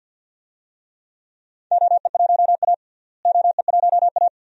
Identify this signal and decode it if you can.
Morse code, heard as a keyed tone, about 46 wpm, 700 Hz. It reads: OE1A OE1A
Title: DXCC-100 Call Signs - 4 Characters - Repeated 2x - 50wpm